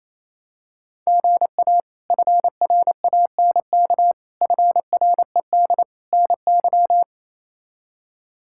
Morse code, heard as a keyed tone, about 27 wpm, 700 Hz. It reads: GA FRANK FRED NY
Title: K1USN SST QSO Practice - 28wpm